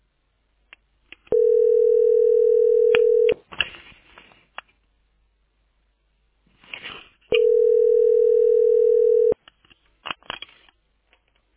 Ringback tone (North America), 440 + 480 Hz.